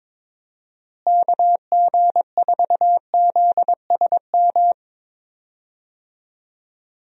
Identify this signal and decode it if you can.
Morse code, heard as a keyed tone, about 22 wpm, 700 Hz. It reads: KG4ZSM
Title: USA Call Signs - 6 Characters - Repeated 2x - 22wpm